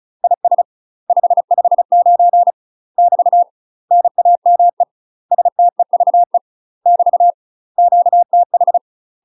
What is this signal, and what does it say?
Morse code, heard as a keyed tone, about 34 wpm, 700 Hz. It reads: IS 559 <BT> NAME STEVE <BT> QTH
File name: UK MASTER QSOs TRAINING / 35WPM / UK MASTER QSOs 0025